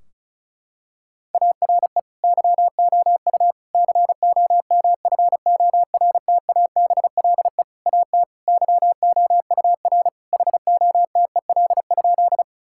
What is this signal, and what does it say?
Morse code, heard as a keyed tone, about 34 wpm, 700 Hz. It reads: ARE YOU COMFORTABLE AT YOUR HOTEL?
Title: Sentences from Top 1600 Words 35wpm